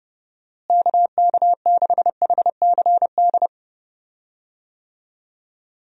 Morse code, heard as a keyed tone, about 29 wpm, 700 Hz. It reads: KK6HCD
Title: USA Call Signs - 6 Characters - Repeated 3x - 30wpm